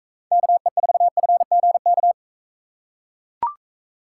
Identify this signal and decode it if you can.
Morse code, heard as a keyed tone, about 43 wpm, 700 Hz. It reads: KE4FGK